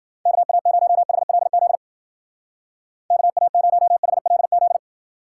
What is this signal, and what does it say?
Morse code, heard as a keyed tone, about 52 wpm, 700 Hz. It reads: KA0HLZ KA0HLZ